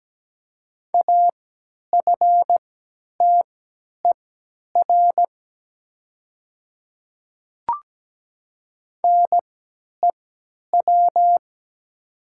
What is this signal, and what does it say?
Morse code, heard as a keyed tone, about 17 wpm, spaced as for 10 wpm, 700 Hz. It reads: AFTER NEW
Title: 3x Character Spacing - Top 200 Words - 17wpm